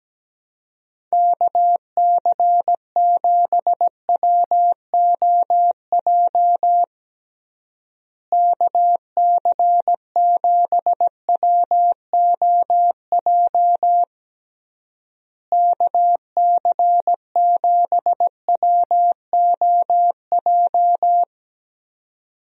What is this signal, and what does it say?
Morse code, heard as a keyed tone, about 17 wpm, 700 Hz. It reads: KC7WOJ KC7WOJ KC7WOJ
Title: USA Call Signs - 6 Characters - Repeated 3x - 17wpm